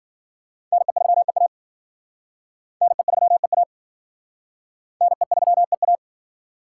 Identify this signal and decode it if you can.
Morse code, heard as a keyed tone, about 52 wpm, 700 Hz. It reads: NE3EA NE3EA NE3EA